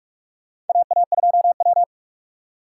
Morse code, heard as a keyed tone, about 46 wpm, 700 Hz. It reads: AA2W
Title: USA Call Signs - 4 Characters - 3x Character Spacing - 45wpm